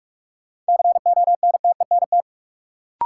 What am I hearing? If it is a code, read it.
Morse code, heard as a keyed tone, about 45 wpm, 700 Hz. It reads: CONTENT
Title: Sentences from Top 900 Words - New Word Drill - 45wpm